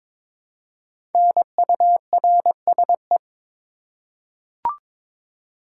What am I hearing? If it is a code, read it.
Morse code, heard as a keyed tone, about 22 wpm, 700 Hz. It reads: NURSE
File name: 5 Letter Words - First 3 Letters Given - 22wpm